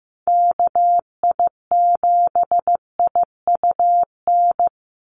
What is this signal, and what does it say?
Morse code, heard as a keyed tone, about 15 wpm, 700 Hz. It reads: KI7IUN
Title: USA Call Signs 15wpm